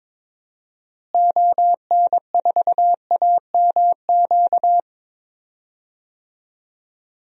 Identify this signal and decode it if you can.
Morse code, heard as a keyed tone, about 22 wpm, 700 Hz. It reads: ON4AMQ
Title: DXCC-100 Call Signs - 6 Characters - Repeated 2x - 22wpm